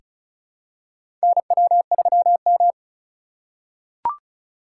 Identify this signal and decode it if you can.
Morse code, heard as a keyed tone, about 36 wpm, 700 Hz. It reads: NW3M
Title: USA Call Signs - 4 Characters - 8x Character Spacing - 35wpm